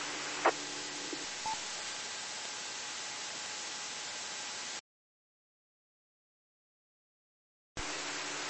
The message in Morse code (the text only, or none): none